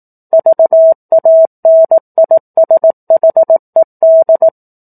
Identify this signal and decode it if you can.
Morse code, heard as a keyed tone, about 18 wpm, 650 Hz. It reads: VANISHED
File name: 04-Chapter-04 18 WPM